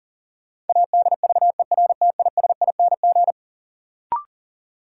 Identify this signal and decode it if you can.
Morse code, heard as a keyed tone, about 40 wpm, 700 Hz. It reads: ADVERTISING
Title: Sentences from Top 1000 Words - Review Part 6 - New Word Drill - 40wpm